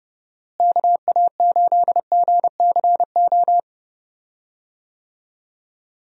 Morse code, heard as a keyed tone, about 30 wpm, 700 Hz. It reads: KA8GCO
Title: USA Call Signs - 6 Characters - Repeated 2x - 30wpm